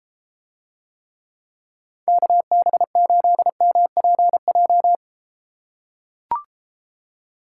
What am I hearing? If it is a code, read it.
Morse code, heard as a keyed tone, about 33 wpm, 700 Hz. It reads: KB8MPJ